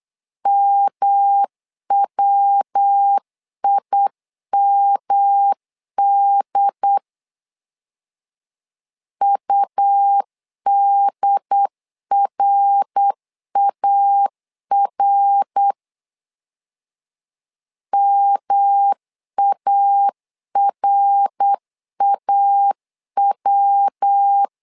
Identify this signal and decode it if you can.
Morse code, heard as a keyed tone, about 8 wpm, 800 Hz. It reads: MWIMD UDRAR MARAW
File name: LEZIONI / 3 MWRO / 3 MWR 8wpm 2444